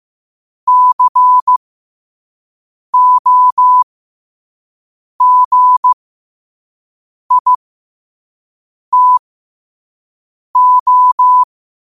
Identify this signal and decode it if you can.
Morse code, heard as a keyed tone, about 15 wpm, spaced as for 5 wpm, 1000 Hz. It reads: COGITO
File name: qotdfiles / 5wpm